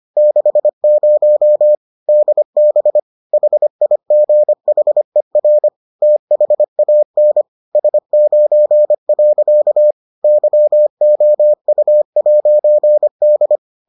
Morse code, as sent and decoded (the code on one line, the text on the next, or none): -.... ----- / -.. -... / .... .. --. .... .-. / - .... .- -. / ... ----. .-.-.- / -.-- --- ..- .----. -..
60 DB HIGHER THAN S9. YOU'D